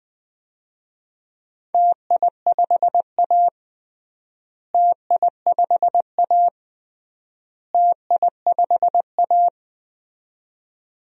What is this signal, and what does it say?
Morse code, heard as a keyed tone, about 20 wpm, 700 Hz. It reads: TI5A TI5A TI5A
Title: DXCC-100 Call Signs - 4 Characters - Repeated 3x - 20wpm